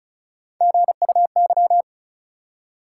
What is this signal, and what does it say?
Morse code, heard as a keyed tone, about 36 wpm, 700 Hz. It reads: GUY
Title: Three Letter Words - 35wpm